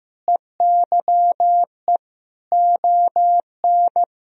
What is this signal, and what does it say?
Morse code, heard as a keyed tone, about 16 wpm, 700 Hz. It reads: EYE ON
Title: Sentences from Top 1000 Words - Review Part 1 - 15wpm